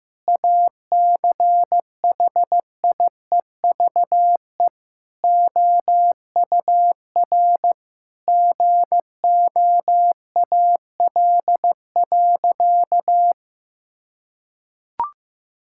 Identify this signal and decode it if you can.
Morse code, heard as a keyed tone, about 16 wpm, 700 Hz. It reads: ACHIEVE OUR GOAL.
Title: Sentences from Top 1000 Words - Review Part 6 - 15wpm